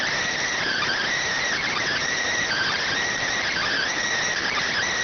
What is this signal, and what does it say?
Emergency Alert System (SAME) tones (header bursts), heard as harsh buzzing bursts. It is heard from 0.2 to 4.7 s.